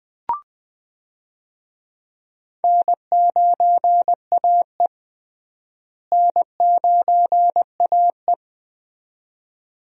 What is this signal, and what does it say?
Morse code, heard as a keyed tone, about 20 wpm, 700 Hz. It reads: N9AE N9AE
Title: USA Call Signs - 4 Characters - Repeated 2x - 20wpm